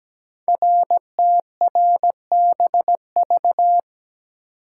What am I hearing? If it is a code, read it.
Morse code, heard as a keyed tone, about 17 wpm, 700 Hz. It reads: RTRBV
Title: Code Groups - 5 Letters - 17wpm